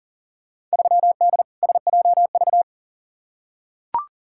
Morse code, heard as a keyed tone, about 40 wpm, 700 Hz. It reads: 3D SJV
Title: Field Day Exchanges - 40wpm